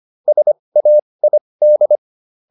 Morse code, heard as a keyed tone, about 25 wpm, spaced as for 20 wpm, 600 Hz. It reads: SAID